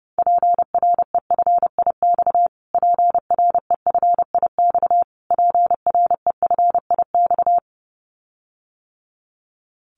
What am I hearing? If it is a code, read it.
Morse code, heard as a keyed tone, about 30 wpm, 700 Hz. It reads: PREFIX PREFIX PREFIX
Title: Callsign Prefixes for Top 25 DXCC - Repeated 3x - 30wpm